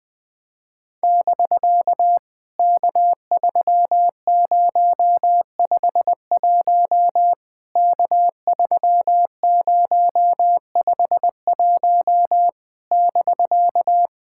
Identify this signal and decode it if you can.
Morse code, heard as a keyed tone, about 21 wpm, 700 Hz. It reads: <BK> K3051 K3051 <BK>
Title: POTA QSO Practice - 20wpm